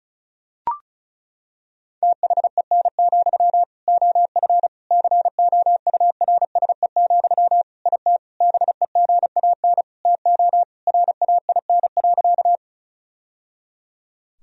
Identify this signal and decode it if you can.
Morse code, heard as a keyed tone, about 34 wpm, 700 Hz. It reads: THEN, OF COURSE, IT BEGAN TO RAIN.
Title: Sentences from Top 1000 Words - Review Part 1 - 35wpm